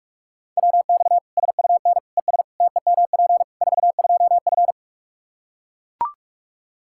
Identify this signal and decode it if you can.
Morse code, heard as a keyed tone, about 41 wpm, 700 Hz. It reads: WX SUN ES TEMP 42F
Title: K1USN SST QSO Practice 45wpm